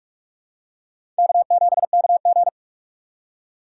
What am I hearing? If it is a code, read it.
Morse code, heard as a keyed tone, about 46 wpm, 700 Hz. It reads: K7KG
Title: USA Call Signs - 4 Characters - 4x Character Spacing - 45wpm